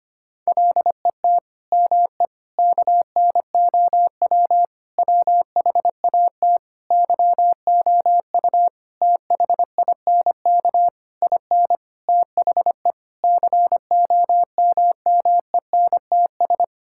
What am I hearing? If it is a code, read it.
Morse code, heard as a keyed tone, about 25 wpm, 700 Hz. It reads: LET ME KNOW WHAT YOU THINK IN THE COMMENTS
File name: practice / 25wpm / Sentences with 9-Word Phrases - 25wpm